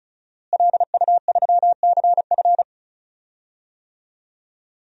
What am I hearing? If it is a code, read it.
Morse code, heard as a keyed tone, about 36 wpm, 700 Hz. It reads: LU3CF
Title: DXCC-100 Call Signs - 5 Characters - Repeated 3x - 35wpm